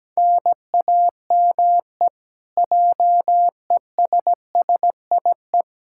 Morse code, heard as a keyed tone, about 17 wpm, 700 Hz. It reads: NAME JESSIE
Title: Practice QSO - Exchange 1 - Beginning - 17wpm